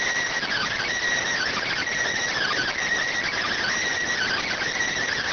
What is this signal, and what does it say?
Emergency Alert System (SAME) tones (header bursts), heard as harsh buzzing bursts. It is heard from 0.4 to 4.7 s.